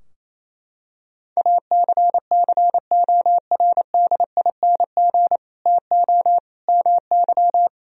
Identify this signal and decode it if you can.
Morse code, heard as a keyed tone, about 28 wpm, 700 Hz. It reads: ACCORDING TO MY
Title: Sentences from Top 1000 Words - Review Part 4 - 28wpm